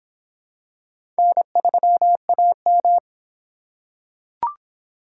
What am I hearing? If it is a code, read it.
Morse code, heard as a keyed tone, about 26 wpm, 700 Hz. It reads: N3AM